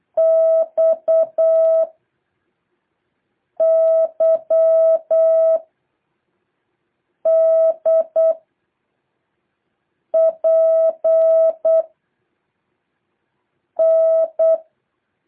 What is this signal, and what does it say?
Morse code, heard as a keyed tone, about 8 wpm, 650 Hz. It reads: X Y D P N
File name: jh8cbh / 2023cw